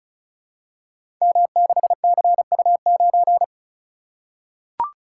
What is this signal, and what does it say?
Morse code, heard as a keyed tone, about 36 wpm, 690 Hz. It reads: M6CU9
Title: Code Groups - 5 Letter:Numbers - 35wpm